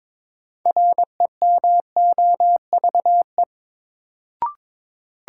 Morse code, heard as a keyed tone, about 22 wpm, 700 Hz. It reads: REMOVE
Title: Sentences from Top 800 Words - New Word Drill - 22wpm